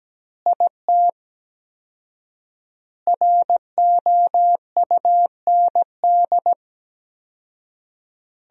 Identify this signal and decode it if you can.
Morse code, heard as a keyed tone, about 17 wpm, 700 Hz. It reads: IT ROUND
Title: 4x Word-Spacing - Sentences from Top 500 Words - 17wpm